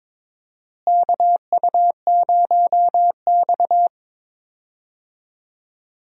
Morse code, heard as a keyed tone, about 22 wpm, 700 Hz. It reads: KU0X
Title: USA Call Signs - 4 Characters - Repeated 2x - 22wpm